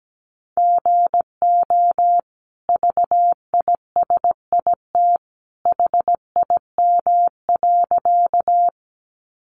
Morse code, heard as a keyed tone, about 17 wpm, 700 Hz. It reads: GO VISIT HIM.
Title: Sentences from Top 900 Words 17wpm